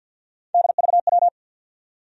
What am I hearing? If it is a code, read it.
Morse code, heard as a keyed tone, about 52 wpm, 700 Hz. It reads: DVW
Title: USA Call Sign Postfixes - 3 Letters 50wpm